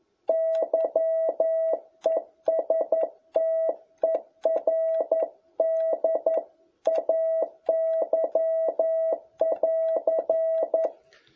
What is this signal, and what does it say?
Morse code, heard as a keyed tone, about 11 wpm, 660 Hz. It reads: YESTERDAY<AR>
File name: jh8cbh / 2021cw